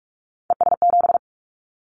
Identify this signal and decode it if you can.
Morse code, heard as a keyed tone, about 44 wpm, 700 Hz. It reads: ES7
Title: Callsign Prefixes for Top 100 DXCC - 45wpm